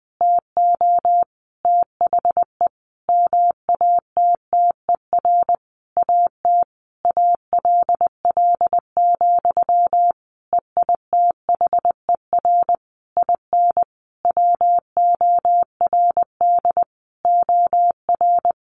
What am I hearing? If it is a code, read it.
Morse code, heard as a keyed tone, about 20 wpm, 700 Hz. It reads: TO THE MATTER AT ALL, EITHER IN WORD OR
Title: Sentences from Top 1200 Words 20wpm